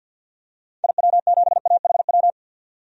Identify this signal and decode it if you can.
Morse code, heard as a keyed tone, about 50 wpm, 700 Hz. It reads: IW7AHW
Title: DXCC-100 Call Signs - 6 Characters - 50wpm